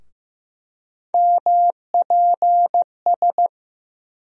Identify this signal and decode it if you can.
Morse code, heard as a keyed tone, about 15 wpm, 700 Hz. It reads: MPS
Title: USA Call Sign Postfixes - 3 Letters 15wpm